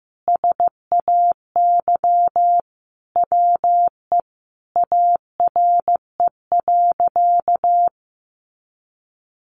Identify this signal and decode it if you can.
Morse code, heard as a keyed tone, about 15 wpm, 700 Hz. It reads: SAY WE ARE.
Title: Sentences from Top 200 Words 15wpm